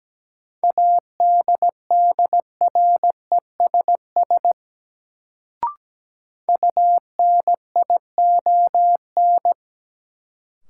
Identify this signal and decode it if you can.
Morse code, heard as a keyed tone, about 17 wpm, 700 Hz. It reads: ADDRESS UNION
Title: Sets of 5 Words Encore - New Word Drill - 17wpm